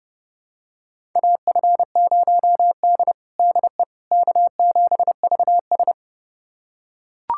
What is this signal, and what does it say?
Morse code, heard as a keyed tone, about 30 wpm, 700 Hz. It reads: AF0D DE K7VS
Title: USA Callsign DE Callsign - 4 Characters - 30wpm